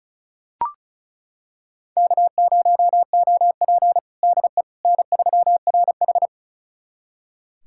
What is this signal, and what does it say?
Morse code, heard as a keyed tone, about 36 wpm, 700 Hz. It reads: K0OP DE N3RH
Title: USA Callsign DE Callsign - 4 Characters - 35wpm